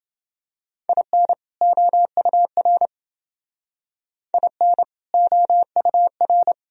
Morse code, heard as a keyed tone, about 28 wpm, 700 Hz. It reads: IN OUR IN OUR